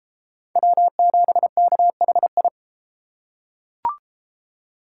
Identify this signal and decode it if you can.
Morse code, heard as a keyed tone, about 32 wpm, 700 Hz. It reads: W7KHI